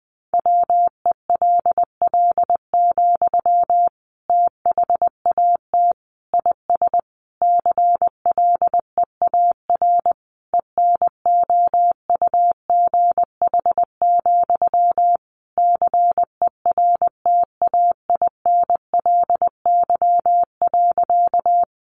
Morse code, heard as a keyed tone, about 20 wpm, 700 Hz. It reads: WELL, THAT IS CLEAR ENOUGH, CERTAINLY.
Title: Sentences from Top 900 Words 20wpm